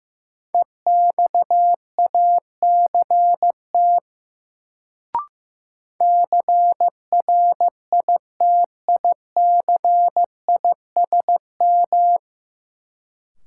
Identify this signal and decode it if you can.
Morse code, heard as a keyed tone, about 15 wpm, 700 Hz. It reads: EXACT CRITICISM
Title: Sentences from Top 1300 Words - New Word Drill - 15wpm